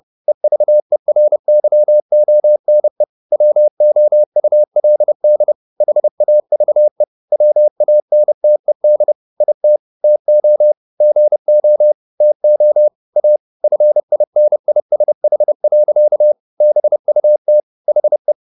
Morse code, heard as a keyed tone, about 30 wpm, 600 Hz. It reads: EVERYONE WOULD HAVE WANTED IT TO GO TO A FINISH. BUT HE
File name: the old man and the sea 2 30wpm